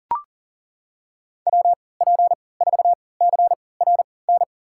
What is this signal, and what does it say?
Morse code, heard as a keyed tone, about 40 wpm, spaced as for 23 wpm, 700 Hz. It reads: WP4CRN
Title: USA Call Signs - 6 Characters - 3x Character Spacing - 40wpm